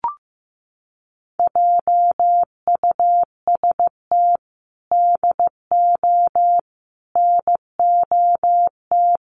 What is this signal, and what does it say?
Morse code, heard as a keyed tone, about 15 wpm, 700 Hz. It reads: JUST DO NOT
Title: Sets of 5 Words Encore 15wpm